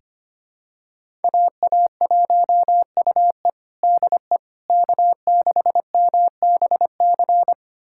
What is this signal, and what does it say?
Morse code, heard as a keyed tone, about 25 wpm, 700 Hz. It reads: AA1UE DE K6MBC
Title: USA Callsign DE Callsign - 5 Characters - 25wpm